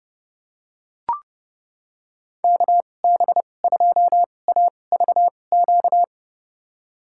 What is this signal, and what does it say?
Morse code, heard as a keyed tone, about 30 wpm, spaced as for 22 wpm, 700 Hz. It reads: KB2AVQ
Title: USA Call Signs - 6 Characters - 2x Character Spacing - 30wpm